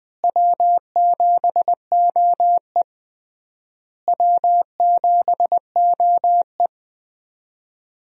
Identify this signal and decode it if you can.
Morse code, heard as a keyed tone, about 20 wpm, 700 Hz. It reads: W7OE W7OE
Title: USA Call Signs - 4 Characters - Repeated 3x - 20wpm